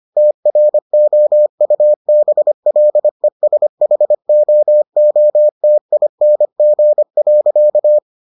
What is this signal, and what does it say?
Morse code, heard as a keyed tone, about 25 wpm, 600 Hz. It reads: TROUBLESHOOTING.